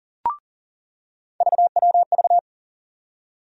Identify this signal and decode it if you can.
Morse code, heard as a keyed tone, about 40 wpm, 700 Hz. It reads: VWV
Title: USA Call Sign Postfixes - 3 Letters 40wpm